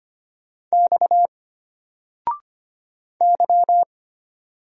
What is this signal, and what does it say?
Morse code, heard as a keyed tone, about 25 wpm, 700 Hz. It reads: X Y
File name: Single-Letter-Number-25wpm